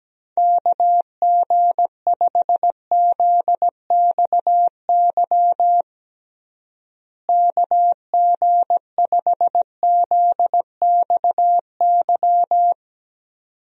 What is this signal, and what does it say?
Morse code, heard as a keyed tone, about 17 wpm, 700 Hz. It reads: KG5ZXY KG5ZXY
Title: USA Call Signs - 6 Characters - Repeated 2x - 17wpm